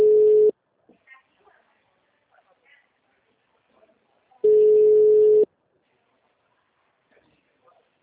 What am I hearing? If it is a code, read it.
Ringback tone (Europe, 425 Hz), with speech.